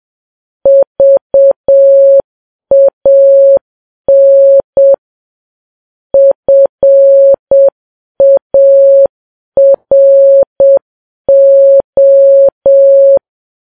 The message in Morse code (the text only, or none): VAN FARO